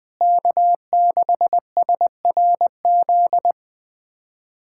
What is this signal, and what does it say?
Morse code, heard as a keyed tone, about 20 wpm, 700 Hz. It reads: K6SRZ
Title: K1USN SST QSO Practice - 20wpm